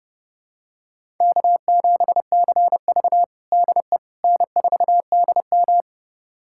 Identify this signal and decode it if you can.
Morse code, heard as a keyed tone, about 30 wpm, 700 Hz. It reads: K7CV DE N4DM
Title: USA Callsign DE Callsign - 4 Characters - 30wpm